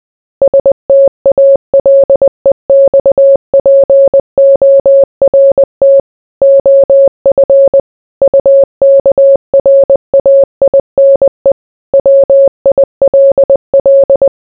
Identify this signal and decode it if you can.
Morse code, heard as a keyed tone, about 20 wpm, 550 Hz. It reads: STALEXPORT OF UKRAINE WILL